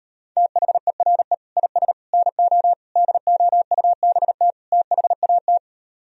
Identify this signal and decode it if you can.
Morse code, heard as a keyed tone, about 38 wpm, 700 Hz. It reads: THERE IS NO DOUBT THAT